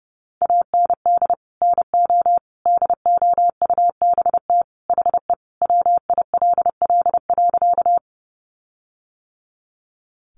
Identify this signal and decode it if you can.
Morse code, heard as a keyed tone, about 30 wpm, 700 Hz. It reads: AND NO DOUBT HE WILL.
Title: Sentences from Top 1500 Words 30wpm